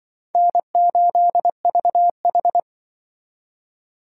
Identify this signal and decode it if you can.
Morse code, heard as a keyed tone, about 25 wpm, 700 Hz. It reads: N8VH
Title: Speed Racing - USA Call Signs 20wpm